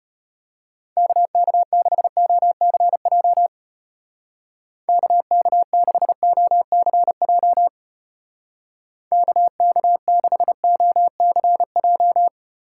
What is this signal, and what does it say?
Morse code, heard as a keyed tone, about 31 wpm, 700 Hz. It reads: KK6OCJ KK6OCJ KK6OCJ
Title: Speed Racing - USA Call Signs 25wpm